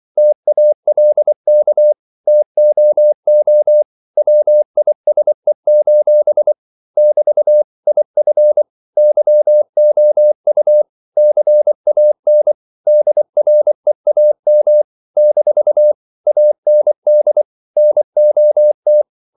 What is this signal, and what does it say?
Morse code, heard as a keyed tone, about 24 wpm, 600 Hz. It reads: TALK TOO WISE: <BT> IF YOU CAN DREAM - AND NOT
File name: CW / Practice / If24